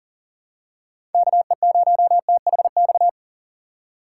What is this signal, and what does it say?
Morse code, heard as a keyed tone, about 38 wpm, 700 Hz. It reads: KE0THX
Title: USA Call Signs - 6 Characters - 2x Character Spacing - 40wpm